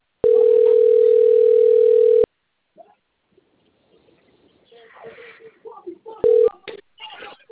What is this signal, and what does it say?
Ringback tone (North America), 440 + 480 Hz, with speech.
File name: out- - - - - .4143